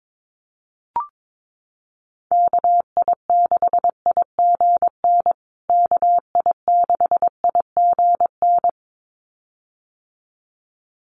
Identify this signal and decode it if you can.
Morse code, heard as a keyed tone, about 22 wpm, 700 Hz. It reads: KI6IGN KI6IGN
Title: USA Call Signs Repeated 2x 22wpm